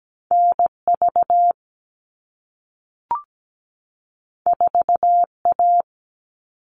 Morse code, heard as a keyed tone, about 17 wpm, 700 Hz. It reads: NV 4A
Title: All Cut Numbers - 2-Digit 17wpm